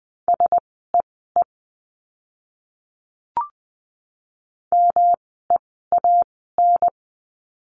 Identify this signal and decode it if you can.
Morse code, heard as a keyed tone, about 20 wpm, spaced as for 14 wpm, 700 Hz. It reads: SEE MEAN
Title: 108 Lesson 18f - Letter m - Words 20wpm